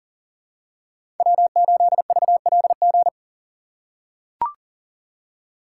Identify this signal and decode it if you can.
Morse code, heard as a keyed tone, about 40 wpm, 700 Hz. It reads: W8VLG
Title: USA Call Signs - 5 Characters - 4x Character Spacing - 40wpm